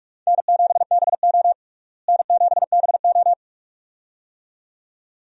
Morse code, heard as a keyed tone, about 42 wpm, 700 Hz. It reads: N7BO N7BO
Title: USA Call Signs - 4 Characters - Repeated 2x - 45wpm